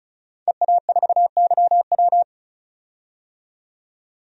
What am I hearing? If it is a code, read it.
Morse code, heard as a keyed tone, about 36 wpm, 700 Hz. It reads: EA4YW
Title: DXCC-100 Call Signs - 5 Characters - 35wpm